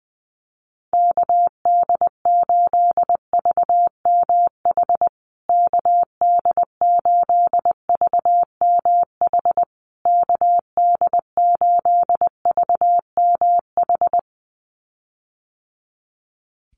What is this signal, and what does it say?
Morse code, heard as a keyed tone, about 19 wpm, 700 Hz. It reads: KD8VMH KD8VMH KD8VMH
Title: USA Call Signs Repeated 3x 20wpm